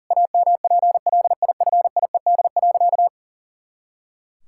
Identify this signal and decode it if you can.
Morse code, heard as a keyed tone, about 40 wpm, 700 Hz. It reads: AMPLIFIED.
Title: Pangram Challenge v2 - 40wpm